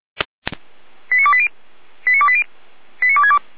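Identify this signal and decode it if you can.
Five-tone selective call (ZVEI), heard as quick runs of five tones, tones of about 70 ms. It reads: 89280 89280 89262
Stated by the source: ZVEI five-tone selective call received by SDR.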